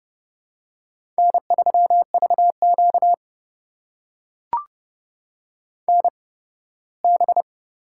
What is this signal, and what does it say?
Morse code, heard as a keyed tone, about 30 wpm, spaced as for 18 wpm, 700 Hz. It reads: N3VQ NB
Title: USA Call Signs - 4 Characters - 8x Character Spacing - 30wpm